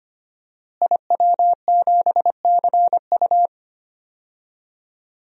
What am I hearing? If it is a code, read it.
Morse code, heard as a keyed tone, about 25 wpm, 700 Hz. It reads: IW7CU
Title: DXCC-100 Call Signs - 5 Characters - Repeated 2x - 25wpm